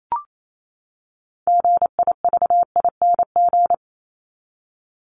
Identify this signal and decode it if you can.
Morse code, heard as a keyed tone, about 28 wpm, 700 Hz. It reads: GIVING
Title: ING Suffix - Top 100 Words with Base Words 28wpm